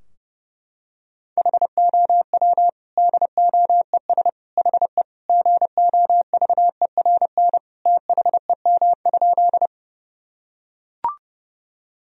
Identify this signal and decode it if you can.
Morse code, heard as a keyed tone, about 32 wpm, 700 Hz. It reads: HOW DOES HE GOVERN THEM?
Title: Sentences from Top 1000 Words - Review Part 2 - 30wpm